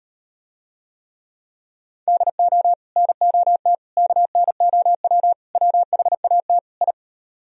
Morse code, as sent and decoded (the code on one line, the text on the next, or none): -.. --- / -. --- - / -.- -. --- .-- / .-- .... .- - / ..
DO NOT KNOW WHAT I